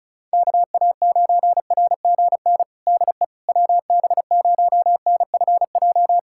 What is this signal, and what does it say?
Morse code, heard as a keyed tone, about 34 wpm, 690 Hz. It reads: KA9RGN DE WB0NFJ
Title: USA Callsign DE Callsign - 6 Characters - 35wpm